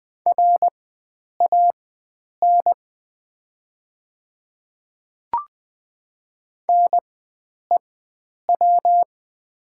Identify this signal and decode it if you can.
Morse code, heard as a keyed tone, about 20 wpm, spaced as for 9 wpm, 700 Hz. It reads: RAN NEW
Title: 4x Character Spacing - Top 500 Words - 20wpm